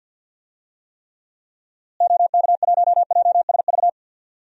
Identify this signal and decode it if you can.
Morse code, heard as a keyed tone, about 51 wpm, 700 Hz. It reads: OK1JSV
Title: DXCC-100 Call Signs - 6 Characters - Repeated 2x - 50wpm